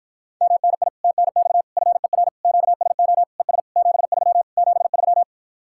Morse code, heard as a keyed tone, about 52 wpm, 700 Hz. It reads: MNI TNX FER QSO ES 73 73